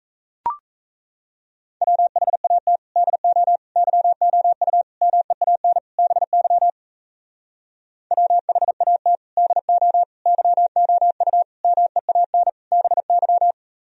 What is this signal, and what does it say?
Morse code, heard as a keyed tone, about 38 wpm, 700 Hz. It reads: WHAT DO YOU MEAN BY WHAT DO YOU MEAN BY